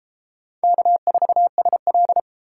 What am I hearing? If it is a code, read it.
Morse code, heard as a keyed tone, about 33 wpm, 700 Hz. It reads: K4SL